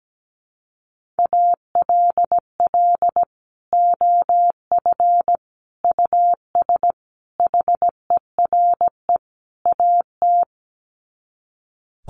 Morse code, heard as a keyed tone, about 17 wpm, 700 Hz. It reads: ALL OF US HERE AT
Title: Sets of 5 Words 17wpm